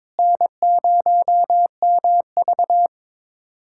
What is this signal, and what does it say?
Morse code, heard as a keyed tone, about 21 wpm, 700 Hz. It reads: N0MV
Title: USA Callsign DE Callsign - 4 Characters - 22wpm